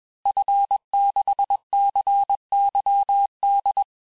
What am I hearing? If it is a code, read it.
Morse code, heard as a keyed tone, about 21 wpm, 800 Hz. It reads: F6CYD
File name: teste-FIN-de-QSO-vitesse-20mpm